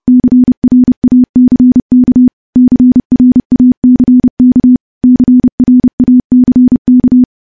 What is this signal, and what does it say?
Morse code, heard as a keyed tone, about 32 wpm, 250 Hz. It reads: CRACK CRACK CRACK
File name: morse / morse-training-30WPM